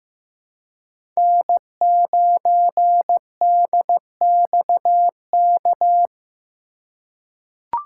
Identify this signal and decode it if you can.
Morse code, heard as a keyed tone, about 16 wpm, 700 Hz. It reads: N9DXK
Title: USA Call Signs - 5 Characters - 2x Character Spacing - 15wpm